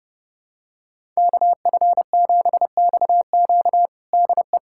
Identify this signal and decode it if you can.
Morse code, heard as a keyed tone, about 30 wpm, 700 Hz. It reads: KF7XQ DE
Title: USA Callsign DE Callsign - 5 Characters - 30wpm